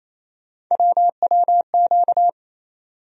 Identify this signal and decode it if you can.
Morse code, heard as a keyed tone, about 28 wpm, 700 Hz. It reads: WWQ
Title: USA Call Sign Postfixes - 3 Letters 28wpm